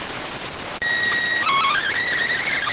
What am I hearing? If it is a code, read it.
SSTV (slow-scan television), PD 290.